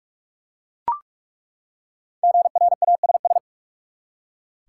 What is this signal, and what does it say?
Morse code, heard as a keyed tone, about 43 wpm, 700 Hz. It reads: GRASS
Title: Five Letter Words - 45wpm